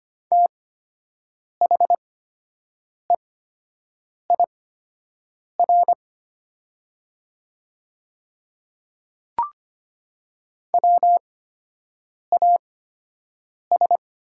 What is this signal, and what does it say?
Morse code, heard as a keyed tone, about 25 wpm, spaced as for 7 wpm, 700 Hz. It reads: THEIR WAS
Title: 8x Character Spacing - Top 200 Words - 25wpm